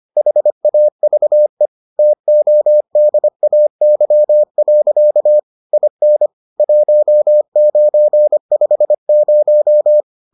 Morse code, heard as a keyed tone, about 25 wpm, 600 Hz. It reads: HAVE TODAY. IN 1950